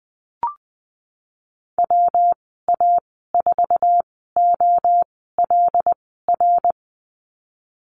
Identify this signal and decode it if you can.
Morse code, heard as a keyed tone, about 20 wpm, spaced as for 14 wpm, 700 Hz. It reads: WA4OLR
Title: 140 Lesson 23g - Letter f - Call Signs 20wpm